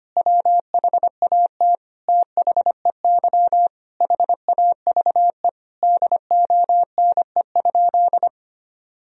Morse code, heard as a keyed tone, about 25 wpm, 700 Hz. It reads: WHAT THEY HAVE DONE?
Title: Sentences from Top 1000 Words - Review Part 3 - 25wpm